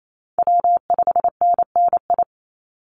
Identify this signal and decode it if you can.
Morse code, heard as a keyed tone, about 28 wpm, 700 Hz. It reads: W5NNI
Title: USA Call Signs 28wpm